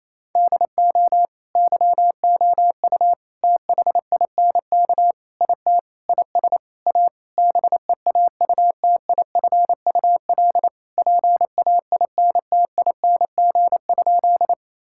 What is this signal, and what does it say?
Morse code, heard as a keyed tone, about 28 wpm, 700 Hz. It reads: DO YOU THINK IT IS A BEAUTIFUL PAINTING?
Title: Sentences from Top 1000 Words - Review Part 6 - 28wpm